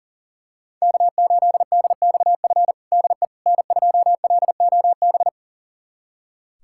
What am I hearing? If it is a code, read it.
Morse code, heard as a keyed tone, about 40 wpm, 700 Hz. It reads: K8DXF DE N2LOB
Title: USA Callsign DE Callsign - 5 Characters - 40wpm